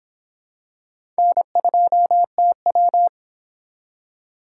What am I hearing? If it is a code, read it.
Morse code, heard as a keyed tone, about 25 wpm, 700 Hz. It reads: N2TW